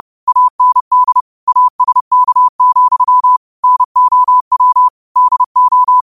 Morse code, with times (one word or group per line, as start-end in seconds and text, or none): 0.3-1.2 AND
1.5-3.4 ASK,
3.6-4.9 NOW
5.2-6.0 DO